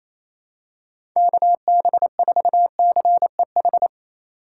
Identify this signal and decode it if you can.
Morse code, heard as a keyed tone, about 28 wpm, 700 Hz. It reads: KB4CEH
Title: USA Call Signs - 6 Characters - 4x Character Spacing - 28wpm